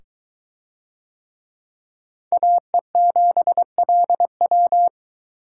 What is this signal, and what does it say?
Morse code, heard as a keyed tone, about 22 wpm, 700 Hz. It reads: AE7LW